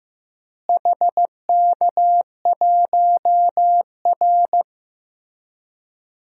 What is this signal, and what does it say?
Morse code, heard as a keyed tone, about 15 wpm, 700 Hz. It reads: HK1R
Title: DXCC-100 Call Signs - 4 Characters - Repeated 3x - 15wpm